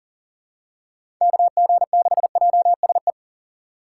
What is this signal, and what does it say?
Morse code, heard as a keyed tone, about 39 wpm, 700 Hz. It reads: KG6JSE
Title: USA Call Signs - 6 Characters - 2x Character Spacing - 40wpm